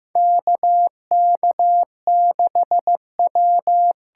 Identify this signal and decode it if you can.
Morse code, heard as a keyed tone, about 15 wpm, 700 Hz. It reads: KK6W
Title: USA Call Signs - 8x Character Spacing - 15wpm